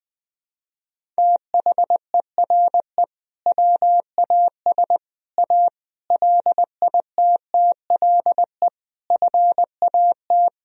Morse code, heard as a keyed tone, about 21 wpm, 700 Hz. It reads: THERE WAS A LITTLE FAT
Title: Sentences from Top 1000 Words - Review Part 4 - 20wpm